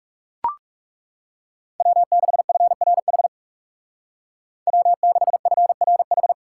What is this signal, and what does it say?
Morse code, heard as a keyed tone, about 37 wpm, 700 Hz. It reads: W6FRH W6FRH
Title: Speed Racing - USA 1x3 Call Signs 30wpm